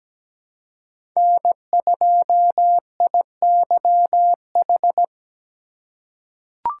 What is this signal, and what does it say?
Morse code, heard as a keyed tone, about 17 wpm, 700 Hz. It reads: N2IYH
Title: USA Call Signs - 5 Characters - 3x Character Spacing - 17wpm